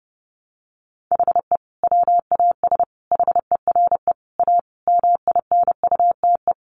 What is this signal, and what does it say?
Morse code, heard as a keyed tone, about 30 wpm, 700 Hz. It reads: HE WAS HERE A MINUTE
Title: Sentences from Top 500 Words 30wpm